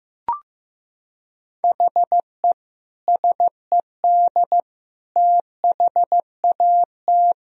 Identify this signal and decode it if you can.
Morse code, heard as a keyed tone, about 15 wpm, 700 Hz. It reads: HE SED THAT
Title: Sentences from Top 200 Words and Abbreviations 15wpm